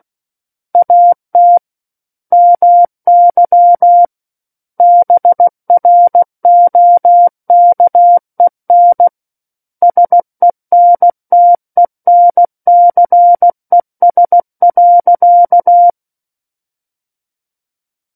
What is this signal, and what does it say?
Morse code, heard as a keyed tone, about 16 wpm, 700 Hz. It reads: AT MY BROKEN SENTENCES.
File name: Book One / 07-Chapter 7 - 16 WPM